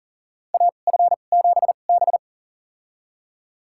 Morse code, heard as a keyed tone, about 40 wpm, spaced as for 29 wpm, 700 Hz. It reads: AF7B
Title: USA Call Signs - 4 Characters - 2x Character Spacing - 40wpm